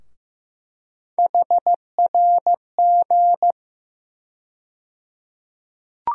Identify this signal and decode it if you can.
Morse code, heard as a keyed tone, about 15 wpm, 700 Hz. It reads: HRG
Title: USA Call Sign Postfixes - 3 Letters 15wpm